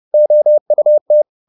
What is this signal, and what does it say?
Morse code, heard as a keyed tone, about 30 wpm, 600 Hz. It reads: OUT
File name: the old man and the sea / the old man and the sea 1 30wpm